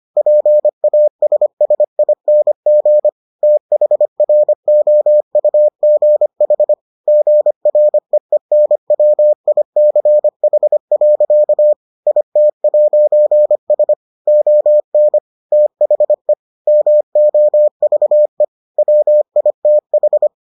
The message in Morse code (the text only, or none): PASSING THROUGH GREENWICH. IT'S ON THE MOVE WITH